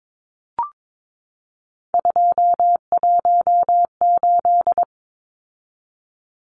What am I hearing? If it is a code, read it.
Morse code, heard as a keyed tone, about 22 wpm, 700 Hz. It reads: 218
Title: Numbers - 3-Digit 22wpm